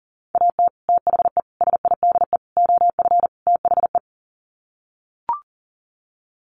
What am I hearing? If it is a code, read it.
Morse code, heard as a keyed tone, about 40 wpm, 700 Hz. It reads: AT THE SIDE OF THE
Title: Sets of 5 Words Encore2 40wpm